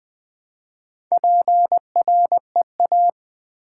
Morse code, heard as a keyed tone, about 20 wpm, 700 Hz. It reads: PREA
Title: Build A Word - 2 Letters At A Time - 6-Letter Words - 20wpm